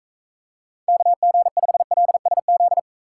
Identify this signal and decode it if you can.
Morse code, heard as a keyed tone, about 43 wpm, 700 Hz. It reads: KG5LSZ